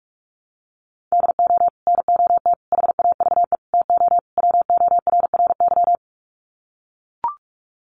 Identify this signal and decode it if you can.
Morse code, heard as a keyed tone, about 44 wpm, 700 Hz. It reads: DO NOT HAVE TO WORRY
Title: Sets of 5 Words 45wpm